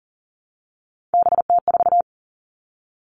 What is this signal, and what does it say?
Morse code, heard as a keyed tone, about 40 wpm, 700 Hz. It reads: BT4
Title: All Cut Numbers - 3-Digit 40wpm